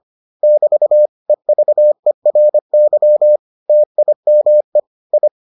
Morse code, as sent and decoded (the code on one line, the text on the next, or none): -...- / . ...- . .-. -.-- / - .. -- . / ..
<BT> EVERY TIME I